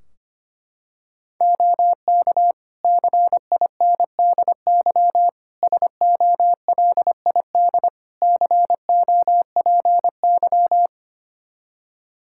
Morse code, heard as a keyed tone, about 25 wpm, 700 Hz. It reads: OK CINDY SOLID COPY
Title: K1USN SST QSO Practice 25wpm